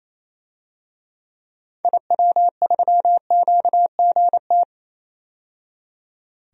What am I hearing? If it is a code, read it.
Morse code, heard as a keyed tone, about 28 wpm, 700 Hz. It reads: IW3QGT
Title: DXCC-100 Call Signs - 6 Characters - 28wpm